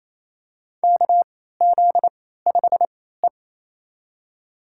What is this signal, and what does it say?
Morse code, heard as a keyed tone, about 28 wpm, spaced as for 16 wpm, 700 Hz. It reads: KZ5E
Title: USA Call Signs - 3x Character Spacing - 28wpm